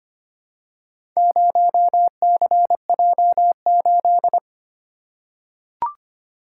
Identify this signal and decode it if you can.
Morse code, heard as a keyed tone, about 25 wpm, 700 Hz. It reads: 0CJ8
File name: practice / 25wpm / Four Letter-Number Combinations - 25wpm